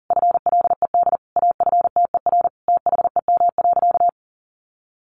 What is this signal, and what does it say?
Morse code, heard as a keyed tone, about 37 wpm, 700 Hz. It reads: FLED AFTER THEM.
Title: A Sound Of Thunder 40wpm